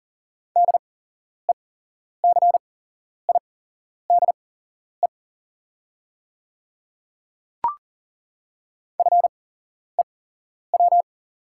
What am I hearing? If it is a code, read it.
Morse code, heard as a keyed tone, about 40 wpm, spaced as for 11 wpm, 690 Hz. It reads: DECIDE FEW
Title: 8x Character Spacing - Top 500 Words - 40wpm